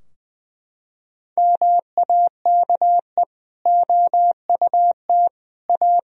Morse code, heard as a keyed tone, about 20 wpm, 700 Hz. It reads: MAKE OUT A
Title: Sentences from Top 1000 Words - Review Part 2 - 20wpm